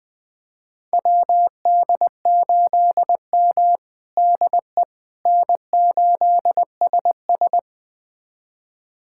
Morse code, heard as a keyed tone, about 20 wpm, 700 Hz. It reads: WD8M DE N8SS
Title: USA Callsign DE Callsign - 4 Characters - 20wpm